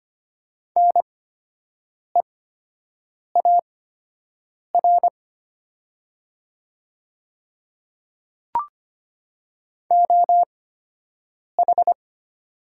Morse code, heard as a keyed tone, about 25 wpm, spaced as for 7 wpm, 700 Hz. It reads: NEAR OH
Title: 8x Character Spacing - Top 500 Words - 25wpm